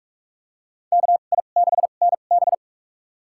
Morse code, heard as a keyed tone, about 46 wpm, spaced as for 33 wpm, 700 Hz. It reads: KI6NB
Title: USA Call Signs - 5 Characters - 2x Character Spacing - 45wpm